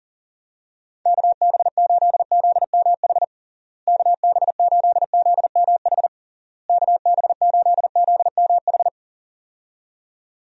Morse code, heard as a keyed tone, about 37 wpm, 700 Hz. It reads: KB8ZMH KB8ZMH KB8ZMH
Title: USA Call Signs - 6 Characters - Repeated 3x - 40wpm